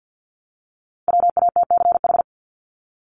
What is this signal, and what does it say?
Morse code, heard as a keyed tone, about 48 wpm, 700 Hz. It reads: WATCH
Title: Top 500 Words Repeated 50wpm